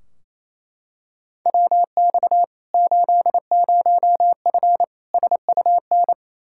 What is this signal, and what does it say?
Morse code, heard as a keyed tone, about 28 wpm, 700 Hz. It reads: WX 80F SUN
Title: Practice QSO - Exchange 2 - Info - 28wpm